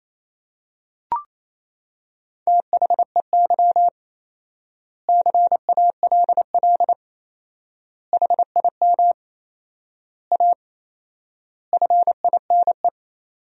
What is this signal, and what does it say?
Morse code, heard as a keyed tone, about 28 wpm, 700 Hz. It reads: THEY CALL HIM A FINE
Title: 4x Word-Spacing - Sentences from Top 500 Words - 28wpm